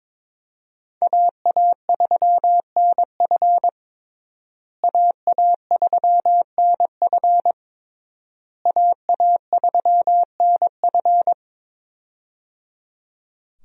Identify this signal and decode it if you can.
Morse code, heard as a keyed tone, about 22 wpm, 700 Hz. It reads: AA3NF AA3NF AA3NF
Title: USA Call Signs - 5 Characters - Repeated 3x - 22wpm